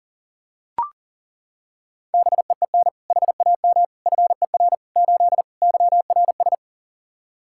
Morse code, heard as a keyed tone, about 40 wpm, 700 Hz. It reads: BEEN HAM FER 8 YRS
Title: Practice QSO - Exchange 3 - More Info - 40wpm